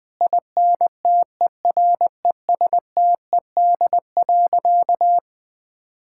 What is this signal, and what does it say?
Morse code, heard as a keyed tone, about 20 wpm, 700 Hz. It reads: INTERESTED.
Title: Sentences from Top 1000 Words - Review Part 6 - 20wpm